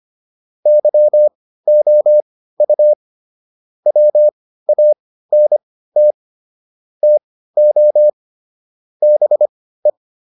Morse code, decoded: YOU WANT TO BE

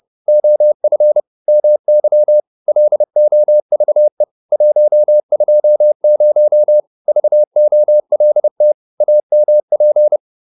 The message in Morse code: OF MY LOVE 120 VOLT AMP

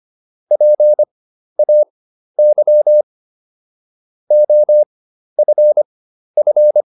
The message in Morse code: PAY OFF

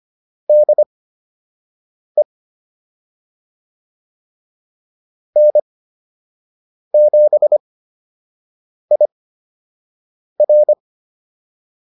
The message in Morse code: DE N7IR

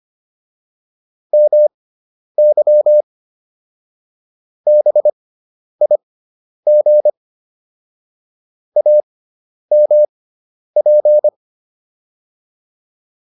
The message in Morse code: MY BIG AMP

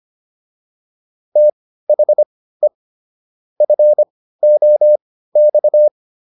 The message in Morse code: THE FOX

